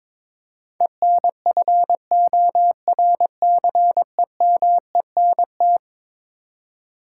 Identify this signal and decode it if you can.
Morse code, heard as a keyed tone, about 22 wpm, 700 Hz. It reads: ENFORCEMENT
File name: practice / 22wpm / Eleven Letter Words - 22wpm